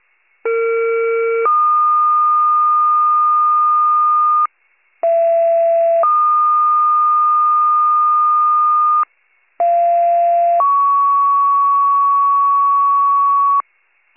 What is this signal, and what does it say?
Two-tone page, heard as a one-second tone, then a three-second tone, about 460 then 1220 Hz, about 670 then 1220 Hz, about 690 then 1120 Hz. It is heard from 0.4 to 13.6 s.